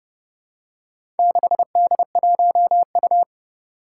Morse code, heard as a keyed tone, about 30 wpm, 700 Hz. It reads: 6D1U
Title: Four Challenging Letter-Number Combinations 30wpm